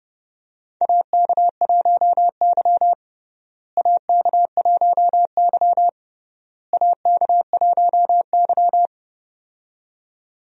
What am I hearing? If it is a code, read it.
Morse code, heard as a keyed tone, about 30 wpm, 700 Hz. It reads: AK1Y AK1Y AK1Y
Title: USA Call Signs - 4 Characters - Repeated 3x - 30wpm